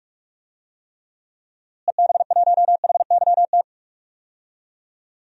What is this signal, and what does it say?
Morse code, heard as a keyed tone, about 42 wpm, 700 Hz. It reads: EB1HYT
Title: DXCC-100 Call Signs - 6 Characters - Repeated 3x - 45wpm